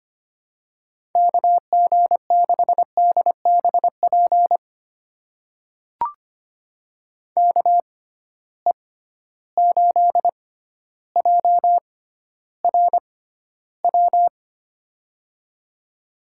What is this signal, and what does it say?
Morse code, heard as a keyed tone, about 25 wpm, spaced as for 13 wpm, 700 Hz. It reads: KG6DBP KE8JRW
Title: USA Call Signs - 6x Character Spacing - 25wpm